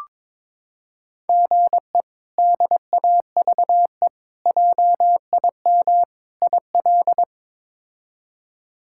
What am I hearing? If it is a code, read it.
Morse code, heard as a keyed tone, about 22 wpm, 700 Hz. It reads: GE DAVE JIM IL
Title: K1USN SST QSO Practice - 22wpm